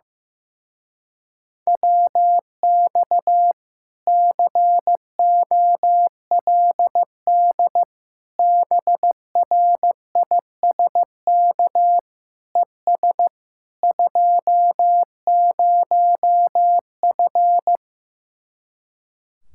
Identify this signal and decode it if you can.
Morse code, heard as a keyed tone, about 15 wpm, 700 Hz. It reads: WX COLD BRISK ES 20F
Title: Speed-Racing Common QSO Phrases Encore - 15wpm